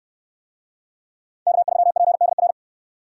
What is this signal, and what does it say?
Morse code, heard as a keyed tone, about 62 wpm, 700 Hz. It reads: K3PDR